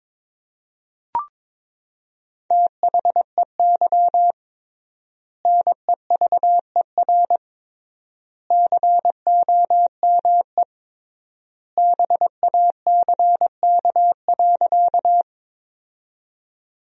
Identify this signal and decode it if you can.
Morse code, heard as a keyed tone, about 22 wpm, 700 Hz. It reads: THEY NEVER COME BACK.
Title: 3x Word-Spacing - Sentences from Top 300 Words - 22wpm